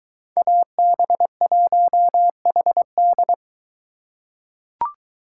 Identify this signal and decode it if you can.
Morse code, heard as a keyed tone, about 22 wpm, 700 Hz. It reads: AB1HD